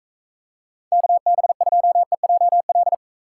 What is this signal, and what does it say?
Morse code, heard as a keyed tone, about 34 wpm, 700 Hz. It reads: KB2EJL
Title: Speed Racing - USA 2x3 Call Signs 28wpm